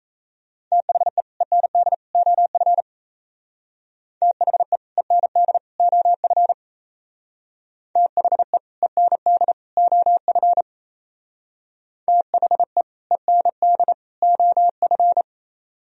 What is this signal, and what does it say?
Morse code, heard as a keyed tone, about 33 wpm, 700 Hz. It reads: THE END OF THE END OF THE END OF THE END OF
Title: Speed-Racing - Sets of 3 Words 28wpm